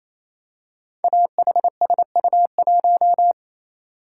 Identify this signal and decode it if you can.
Morse code, heard as a keyed tone, about 29 wpm, 700 Hz. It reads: AHSU1
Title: Code Groups - 5 Letter:Numbers - 28wpm